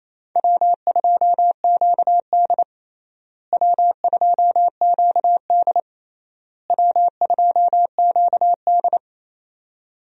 Morse code, heard as a keyed tone, about 28 wpm, 700 Hz. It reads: W2QD W2QD W2QD
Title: USA Call Signs - 4 Characters - Repeated 3x - 28wpm